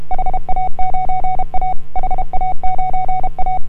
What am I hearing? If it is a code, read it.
Morse code, heard as a keyed tone, about 33 wpm, 740 Hz. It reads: HA9A HA9A
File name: AUDIO2018 / HA9A CW